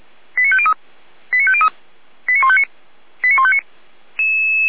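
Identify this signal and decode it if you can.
Five-tone selective call (ZVEI), heard as quick runs of five tones, tones of about 70 ms. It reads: 89503 89503 89159 89159
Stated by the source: ZVEI five-tone call 89099 received on an SDR, fire brigade Feuerwehr Orlamünde.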